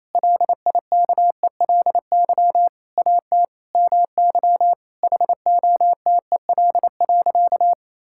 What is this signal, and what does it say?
Morse code, heard as a keyed tone, about 28 wpm, 700 Hz. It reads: LIKELY AT MY HOTEL.
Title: Sentences from Top 1600 Words 28wpm